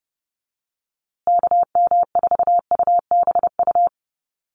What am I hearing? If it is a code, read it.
Morse code, heard as a keyed tone, about 30 wpm, 700 Hz. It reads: KM4UBU
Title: USA Call Signs Repeated 3x 30wpm